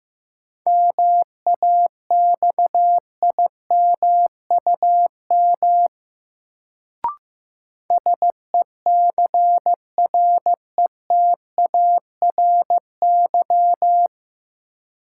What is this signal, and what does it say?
Morse code, heard as a keyed tone, about 15 wpm, 700 Hz. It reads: MAXIMUM SECRETARY
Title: Sentences from Top 1300 Words - New Word Drill - 15wpm